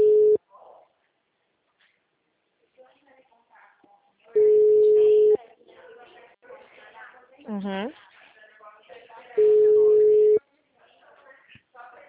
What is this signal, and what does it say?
Ringback tone (Europe, 425 Hz), with speech.
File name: out- -2014- - - .72883